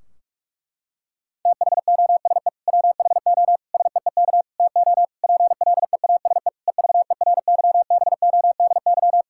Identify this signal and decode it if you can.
Morse code, heard as a keyed tone, about 46 wpm, 700 Hz. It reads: THOSE WHO SEEK TO PLEASE EVERYBODY
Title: Sentences from Top 1600 Words 45wpm